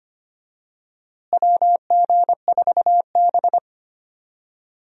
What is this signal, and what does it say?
Morse code, heard as a keyed tone, about 25 wpm, 700 Hz. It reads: WG4B
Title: USA Call Signs - 4 Characters - Repeated 2x - 25wpm